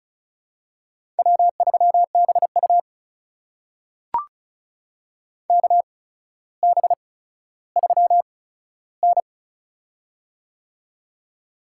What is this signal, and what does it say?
Morse code, heard as a keyed tone, about 36 wpm, spaced as for 15 wpm, 700 Hz. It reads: W3BU KB3N
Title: USA Call Signs - 4 Characters - 8x Character Spacing - 35wpm